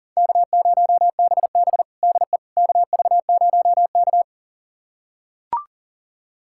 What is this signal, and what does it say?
Morse code, heard as a keyed tone, about 37 wpm, 700 Hz. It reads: K0BB DE KV0K
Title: USA Callsign DE Callsign - 4 Characters - 40wpm